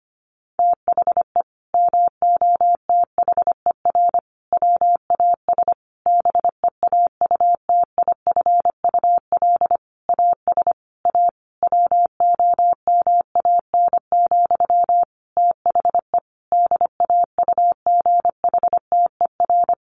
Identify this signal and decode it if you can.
Morse code, heard as a keyed tone, about 25 wpm, 700 Hz. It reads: THE MOTHER WAS BEAUTIFUL AS A WOMAN, THE DAUGHTER
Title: Sentences from Top 1500 Words 25wpm